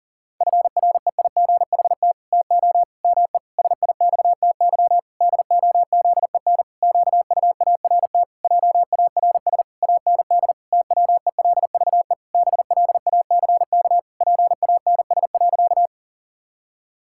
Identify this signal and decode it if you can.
Morse code, heard as a keyed tone, about 40 wpm, 700 Hz. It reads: FREIGHT TO ME SIXTY DOZEN QUART JARS AND TWELVE BLACK PANS.
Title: Pangram Challenge v2 - 40wpm